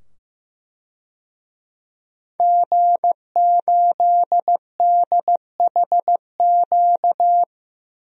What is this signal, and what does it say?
Morse code, heard as a keyed tone, about 15 wpm, 700 Hz. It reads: G8DHQ
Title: DXCC-100 Call Signs - 5 Characters - 15wpm